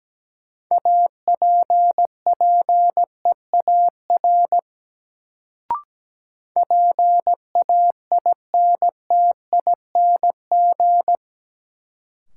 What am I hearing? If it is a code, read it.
Morse code, heard as a keyed tone, about 17 wpm, 700 Hz. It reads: APPEAR PAINTING
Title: Sentences from Top 1200 Words - New Word Drill Alt - 17wpm